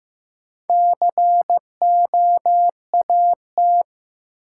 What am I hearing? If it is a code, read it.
Morse code, heard as a keyed tone, about 15 wpm, 700 Hz. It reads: COAT
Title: Sentences from Top 1600 Words - New Word Drill Alt - 15wpm